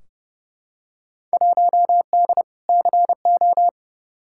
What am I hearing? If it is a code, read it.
Morse code, heard as a keyed tone, about 30 wpm, 700 Hz. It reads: 1D CO
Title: Field Day Exchanges - 30wpm